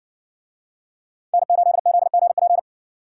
Morse code, heard as a keyed tone, about 60 wpm, 700 Hz. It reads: N8ZGP